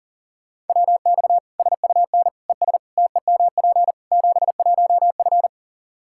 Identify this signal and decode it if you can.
Morse code, heard as a keyed tone, about 37 wpm, 700 Hz. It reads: WX SUN ES TEMP 71F
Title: Practice QSO - Exchange 2 - Info - 40wpm